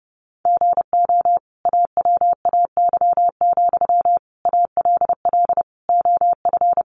Morse code, heard as a keyed tone, about 30 wpm, 700 Hz. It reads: GO AWAY, ALL OF
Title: Sentences from Top 600 Words 30wpm